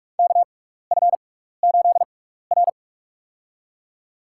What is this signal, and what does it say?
Morse code, heard as a keyed tone, about 46 wpm, spaced as for 15 wpm, 690 Hz. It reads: KF8R
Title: USA Call Signs - 6x Character Spacing - 45wpm